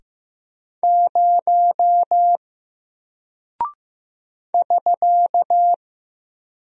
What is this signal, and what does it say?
Morse code, heard as a keyed tone, about 15 wpm, 700 Hz. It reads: 0 <SK>
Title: Single Character (Letter:Number:Punctuation:Prosign) Alt 15wpm